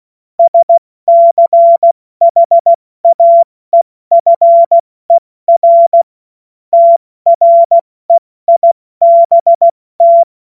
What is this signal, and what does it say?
Morse code, heard as a keyed tone, about 16 wpm, 680 Hz. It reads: SCHAEFER TREIBT
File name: Morsekurs / Witze / Witz 26 Dezember 2023 70 F80